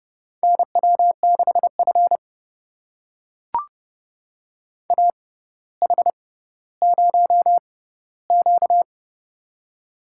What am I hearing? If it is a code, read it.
Morse code, heard as a keyed tone, about 30 wpm, spaced as for 15 wpm, 700 Hz. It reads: NW6F AH0Q
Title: USA Call Signs - 4 Characters - 6x Character Spacing - 30wpm